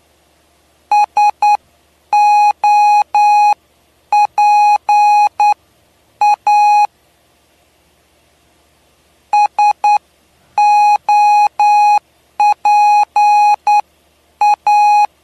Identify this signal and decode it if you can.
Morse code, heard as a keyed tone, about 10 wpm, spaced as for 8 wpm, 830 Hz. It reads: SOPA SOPA
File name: cw / py2lcd / 07-PY2LCD-Li��o 7